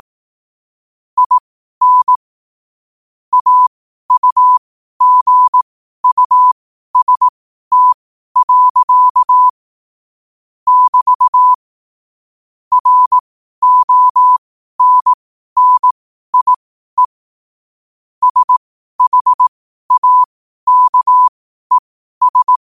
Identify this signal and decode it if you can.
Morse code, heard as a keyed tone, about 18 wpm, spaced as for 12 wpm, 1000 Hz. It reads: IN AUGUST. <BT> RONNIE SHAKES